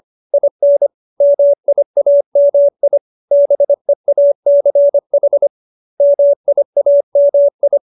Morse code, heard as a keyed tone, about 25 wpm, 570 Hz. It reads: IN MIAMI BEACH MIAMI